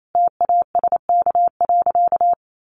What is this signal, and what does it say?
Morse code, heard as a keyed tone, about 28 wpm, 700 Hz. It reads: TASK.
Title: Sentences from Top 1100 Words 28wpm